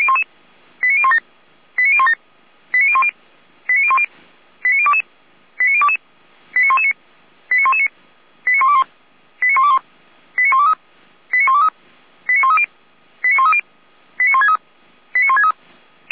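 Five-tone selective call (ZVEI), heard as quick runs of five tones, tones of about 70 ms. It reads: 89017 89017 89010 89010 89022 89022 89119 89119 89121 89121 89123 89123 89130 89130 89163 89163
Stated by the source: ZVEI five-tone call received on an SDR.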